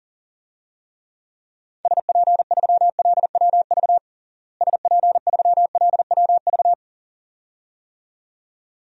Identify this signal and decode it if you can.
Morse code, heard as a keyed tone, about 40 wpm, 700 Hz. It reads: SP3LWV SP3LWV
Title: DXCC-100 Call Signs - 6 Characters - Repeated 2x - 40wpm